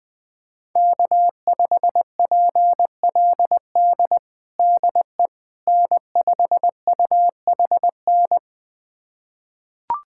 Morse code, heard as a keyed tone, about 20 wpm, 700 Hz. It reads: K5PLD DE N5UHN
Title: USA Callsign DE Callsign - 5 Characters - 20wpm